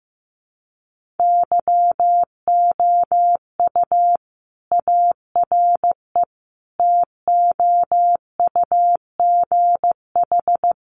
Morse code, heard as a keyed tone, about 15 wpm, 700 Hz. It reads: YOU ARE TOUGH
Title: Sentences from Top 1200 Words 15wpm